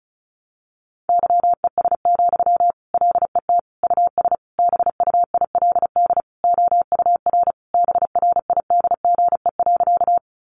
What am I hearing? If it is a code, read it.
Morse code, heard as a keyed tone, about 34 wpm, 700 Hz. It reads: YES, LET US BUILD OUR BRIDGE.
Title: Sentences from Top 1200 Words 35wpm